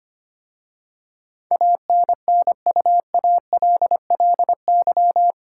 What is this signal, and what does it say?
Morse code, heard as a keyed tone, about 25 wpm, 700 Hz. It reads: ANNUALLY
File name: 8 Letter Words - First 3 Letters Given - 25wpm